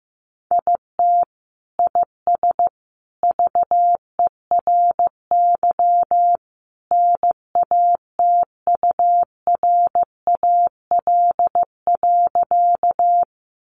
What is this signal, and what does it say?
Morse code, heard as a keyed tone, about 15 wpm, 700 Hz. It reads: IT IS VERY NATURAL.
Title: Sentences from Top 700 Words 15wpm